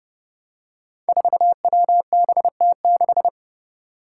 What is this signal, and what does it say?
Morse code, heard as a keyed tone, about 30 wpm, 700 Hz. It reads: 4WBT6
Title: Five Challenging Letter-Number Combinations 30wpm